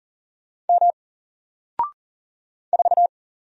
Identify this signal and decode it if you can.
Morse code, heard as a keyed tone, about 40 wpm, 700 Hz. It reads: M 4
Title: Single Letter-Number Alt 40wpm